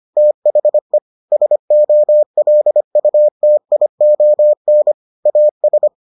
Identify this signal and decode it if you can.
Morse code, heard as a keyed tone, about 25 wpm, 600 Hz. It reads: THE SOLUTION AS